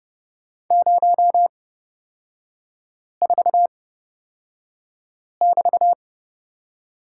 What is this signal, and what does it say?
Morse code, heard as a keyed tone, about 29 wpm, spaced as for 5 wpm, 700 Hz. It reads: 04<BT>
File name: Lesson 36e - BT - Three Characters 30wpm